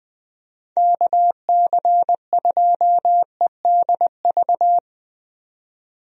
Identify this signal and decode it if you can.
Morse code, heard as a keyed tone, about 20 wpm, 700 Hz. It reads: KC2EDV
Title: USA Call Signs - 6 Characters - 20wpm